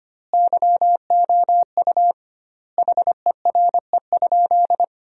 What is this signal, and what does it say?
Morse code, heard as a keyed tone, about 25 wpm, 700 Hz. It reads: YOU HERE?
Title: 2x Word-Spacing - Sentences from Top 200 Words - 25wpm